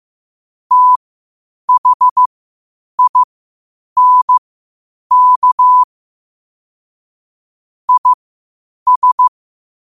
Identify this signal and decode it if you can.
Morse code, heard as a keyed tone, about 15 wpm, spaced as for 8 wpm, 1000 Hz. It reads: THINK IS
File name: qotdfiles / 8wpm